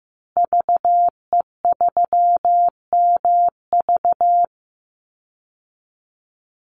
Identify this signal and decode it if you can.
Morse code, heard as a keyed tone, about 14 wpm, 700 Hz. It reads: VE3MV
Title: Call Signs Repeated 3x - 15wpm